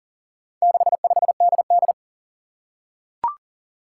Morse code, heard as a keyed tone, about 40 wpm, 700 Hz. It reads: 65DD
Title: Four Challenging Letter-Number Combinations 40wpm